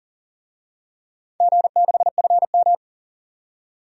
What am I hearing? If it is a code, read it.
Morse code, heard as a keyed tone, about 44 wpm, 700 Hz. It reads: G6FM
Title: DXCC-100 Call Signs - 4 Characters - 40wpm